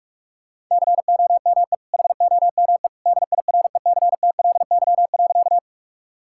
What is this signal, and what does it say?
Morse code, heard as a keyed tone, about 42 wpm, 700 Hz. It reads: COME HOME DIRECTLY.
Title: Sentences from Top 1000 Words - Review Part 5 - 45wpm